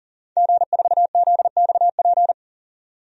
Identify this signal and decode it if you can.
Morse code, heard as a keyed tone, about 39 wpm, 700 Hz. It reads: G4ZXP
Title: DXCC-100 Call Signs - 5 Characters - 40wpm